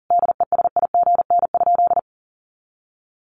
Morse code, heard as a keyed tone, about 40 wpm, 690 Hz. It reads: DESIGN?
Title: Sentences from Top 800 Words 40wpm